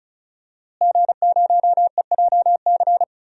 Morse code, heard as a keyed tone, about 34 wpm, 700 Hz. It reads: G0EJC
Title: DXCC-100 Call Signs - 5 Characters - Repeated 3x - 35wpm